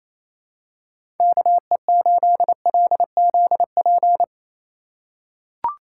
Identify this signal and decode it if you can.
Morse code, heard as a keyed tone, about 28 wpm, 700 Hz. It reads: KE8LZP
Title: USA Call Signs - 6 Characters - 2x Character Spacing - 28wpm